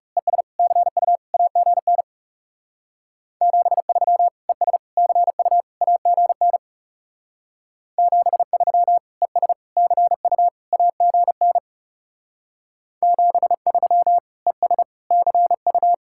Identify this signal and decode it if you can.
Morse code, heard as a keyed tone, about 36 wpm, 700 Hz. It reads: ES CU AGN 73 ES CU AGN 73 ES CU AGN 73 ES CU
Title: Speed-Racing Common QSO Phrases - 30wpm